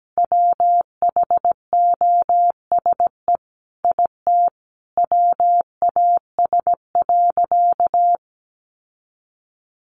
Morse code, heard as a keyed tone, about 17 wpm, 700 Hz. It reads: WHOSE IT WAS.
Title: Sentences from Top 1200 Words 17wpm